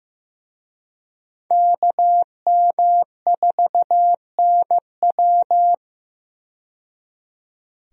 Morse code, heard as a keyed tone, about 15 wpm, 700 Hz. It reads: KM4NW
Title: USA Call Signs - 5 Characters - 15wpm